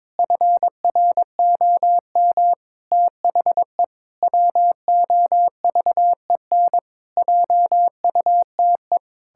Morse code, heard as a keyed tone, about 22 wpm, 700 Hz. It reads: FROM THE WOVEN JUTE
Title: Pangram Challenge v2 - 22wpm